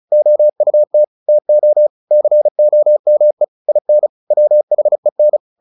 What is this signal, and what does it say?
Morse code, heard as a keyed tone, about 34 wpm, 600 Hz. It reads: OUT TO COME IN WHEN
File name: the old man and the sea / the old man and the sea 1 35wpm